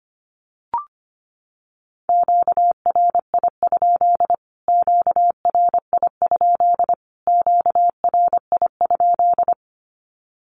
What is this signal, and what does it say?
Morse code, heard as a keyed tone, about 25 wpm, 700 Hz. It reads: QRI? QRI? QRI?
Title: Q-Codes Repeated 3x 25wpm